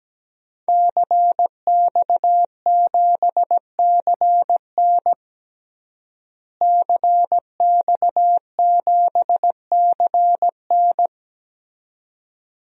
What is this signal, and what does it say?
Morse code, heard as a keyed tone, about 17 wpm, 700 Hz. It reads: CX7CN CX7CN
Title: DXCC-100 Call Signs - 5 Characters - Repeated 2x - 17wpm